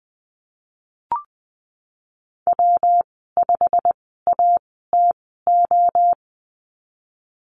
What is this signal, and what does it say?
Morse code, heard as a keyed tone, about 20 wpm, spaced as for 15 wpm, 700 Hz. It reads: W5ATO
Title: 122 Lesson 20g - Number 3 - Call Signs 20wpm